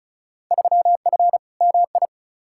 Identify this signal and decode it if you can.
Morse code, heard as a keyed tone, about 36 wpm, 700 Hz. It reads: 3F MI
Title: Field Day Exchanges - 35wpm